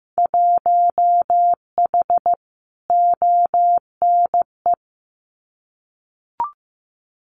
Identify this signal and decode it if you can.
Morse code, heard as a keyed tone, about 15 wpm, 700 Hz. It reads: 1H ONE
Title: Winter Field Day Exchanges 15wpm